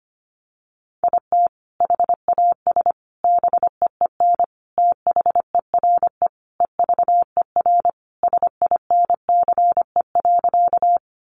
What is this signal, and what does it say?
Morse code, heard as a keyed tone, about 25 wpm, 700 Hz. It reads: IT HAS BEEN THERE EVER SINCE.
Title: Sentences from Top 400 Words 25wpm